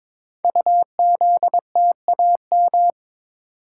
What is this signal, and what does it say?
Morse code, heard as a keyed tone, about 22 wpm, 700 Hz. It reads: UZTAM
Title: Code Groups - 5 Letters - 22wpm